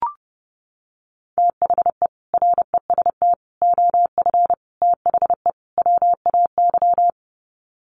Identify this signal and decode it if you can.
Morse code, heard as a keyed tone, about 30 wpm, 700 Hz. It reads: THE REST OF THE WAY
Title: Sets of 5 Words Encore 30wpm